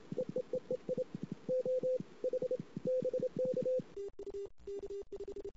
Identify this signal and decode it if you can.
Morse code, heard as a keyed tone, about 27 wpm, 510 Hz. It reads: EEEEI O H BK
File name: CW / 2008-ss-cw-k5tr-op-n5rz -16:00